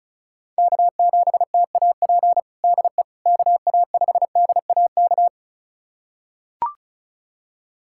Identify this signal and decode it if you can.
Morse code, heard as a keyed tone, about 36 wpm, 690 Hz. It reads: K7TAP DE KA5DAK
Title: USA Callsign DE Callsign 35wpm